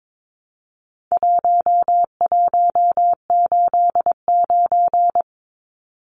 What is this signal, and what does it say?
Morse code, heard as a keyed tone, about 22 wpm, 700 Hz. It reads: 1189
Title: Numbers - 4-Digit 22wpm